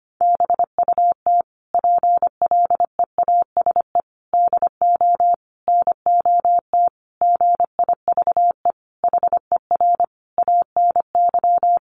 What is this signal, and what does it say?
Morse code, heard as a keyed tone, about 25 wpm, 700 Hz. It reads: BUT PLEASE DO NOT GIVE HER ANY
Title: Sentences from Top 1500 Words 25wpm